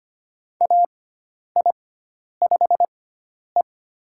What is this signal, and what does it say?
Morse code, heard as a keyed tone, about 25 wpm, spaced as for 10 wpm, 700 Hz. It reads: AI5E
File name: Lesson 16g - Letter c - Call Signs 25wpm